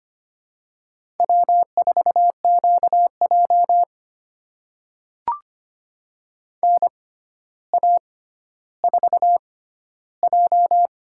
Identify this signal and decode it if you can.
Morse code, heard as a keyed tone, about 25 wpm, spaced as for 13 wpm, 700 Hz. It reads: W4QJ NA4J
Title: USA Call Signs - 4 Characters - 6x Character Spacing - 25wpm